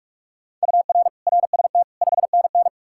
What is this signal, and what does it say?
Morse code, heard as a keyed tone, about 46 wpm, 700 Hz. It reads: UR RST 5NN
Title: Practice QSO - Exchange 1 - Beginning - 45wpm